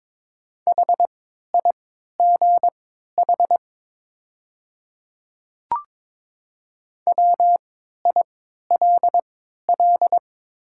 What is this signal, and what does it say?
Morse code, heard as a keyed tone, about 23 wpm, spaced as for 13 wpm, 700 Hz. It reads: HIGH WILL
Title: 3x Character Spacing - Top 200 Words - 22wpm